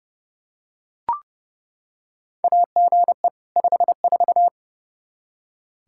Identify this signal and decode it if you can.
Morse code, heard as a keyed tone, about 30 wpm, 700 Hz. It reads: AGE 54
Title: Practice QSO - Exchange 3 - More Info - 30wpm